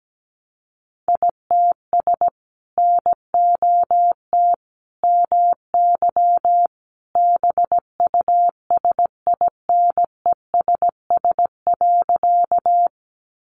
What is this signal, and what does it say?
Morse code, heard as a keyed tone, about 17 wpm, 700 Hz. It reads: ITS NOT MY BUSINESS.
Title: Sentences from Top 600 Words 17wpm